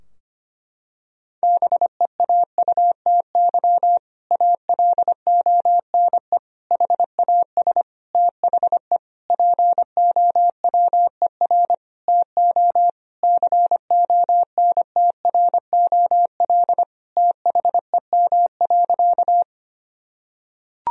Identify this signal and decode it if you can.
Morse code, heard as a keyed tone, about 25 wpm, 700 Hz. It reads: BEAUTY ALONE HAS THE POWER TO CONTROL THEM.
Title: Sentences from Top 1000 Words - Review Part 2 - 25wpm